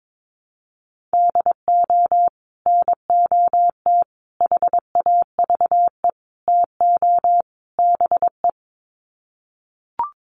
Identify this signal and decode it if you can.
Morse code, heard as a keyed tone, about 22 wpm, 700 Hz. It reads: DO NOT HAVE TO BE
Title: Sets of 5 Words 22wpm